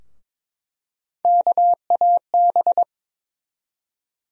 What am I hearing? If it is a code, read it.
Morse code, heard as a keyed tone, about 23 wpm, 700 Hz. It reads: KAB
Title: USA Call Sign Postfixes - 3 Letters 22wpm